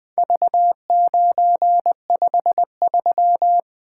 Morse code, heard as a keyed tone, about 24 wpm, 700 Hz. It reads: V953
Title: Speed-Racing - Four Letter-Number Combinations - 20wpm